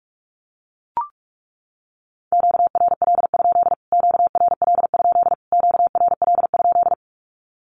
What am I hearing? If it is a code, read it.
Morse code, heard as a keyed tone, about 44 wpm, 700 Hz. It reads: QRL? QRL? QRL?
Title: Q-Codes Repeated 3x 45wpm